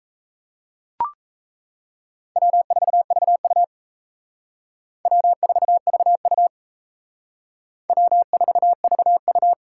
Morse code, heard as a keyed tone, about 35 wpm, 700 Hz. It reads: W4VU W4VU W4VU